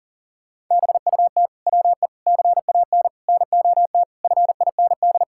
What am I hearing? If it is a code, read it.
Morse code, heard as a keyed tone, about 40 wpm, 700 Hz. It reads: BUT WE CAN NOT FIND
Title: Sentences from Top 1000 Words - Review Part 5 - 40wpm